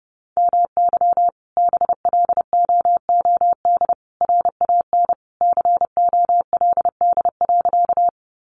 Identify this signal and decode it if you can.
Morse code, heard as a keyed tone, about 30 wpm, 700 Hz. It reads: MY BLOOD RAN COLD.
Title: Sentences from Top 1100 Words 30wpm